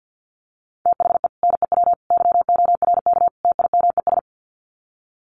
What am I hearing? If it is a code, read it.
Morse code, heard as a keyed tone, about 50 wpm, 700 Hz. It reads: THE NEW YORK TIMES
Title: Sets of 4 Words 50wpm